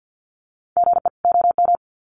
Morse code, heard as a keyed tone, about 50 wpm, 700 Hz. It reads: GE OM
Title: Common QSO Phrases 50wpm